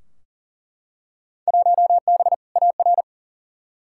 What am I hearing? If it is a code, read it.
Morse code, heard as a keyed tone, about 40 wpm, 700 Hz. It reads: 1B AR
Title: Field Day Exchanges Encore - 40wpm